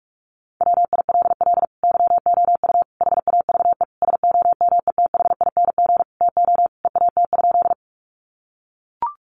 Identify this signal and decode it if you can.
Morse code, heard as a keyed tone, about 44 wpm, 700 Hz. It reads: WILL YOU HAVE SOMETHING TO EAT?
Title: Sentences from Top 600 Words 45wpm